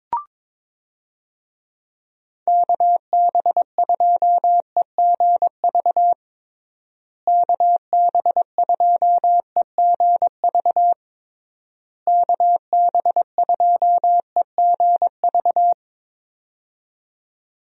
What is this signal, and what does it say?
Morse code, heard as a keyed tone, about 23 wpm, 700 Hz. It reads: KB2EGV KB2EGV KB2EGV
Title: USA Call Signs - 6 Characters - Repeated 3x - 22wpm